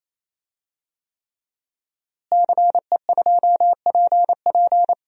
Morse code, heard as a keyed tone, about 28 wpm, 700 Hz. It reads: CE2PP
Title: DXCC-100 Call Signs - 5 Characters - Repeated 3x - 28wpm